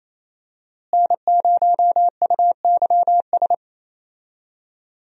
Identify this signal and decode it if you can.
Morse code, heard as a keyed tone, about 28 wpm, 700 Hz. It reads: N0UYS
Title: USA Call Signs - 5 Characters - Repeated 3x - 28wpm